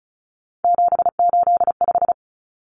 Morse code, heard as a keyed tone, about 34 wpm, 700 Hz. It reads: 785
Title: Numbers - 3-Digit 35wpm